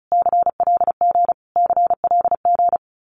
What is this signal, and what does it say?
Morse code, heard as a keyed tone, about 34 wpm, 700 Hz. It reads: CLG CLG
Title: Abbreviations Repeated 3x - 35wpm